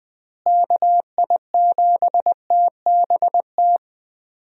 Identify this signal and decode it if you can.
Morse code, heard as a keyed tone, about 20 wpm, 700 Hz. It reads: KI7TBT
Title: USA Call Signs - 6 Characters - Repeated 3x - 20wpm